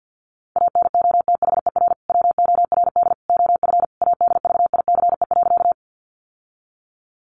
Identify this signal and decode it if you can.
Morse code, heard as a keyed tone, about 50 wpm, 700 Hz. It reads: ANOTHER WORD OF ADVICE.
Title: Sentences from Top 1100 Words 50wpm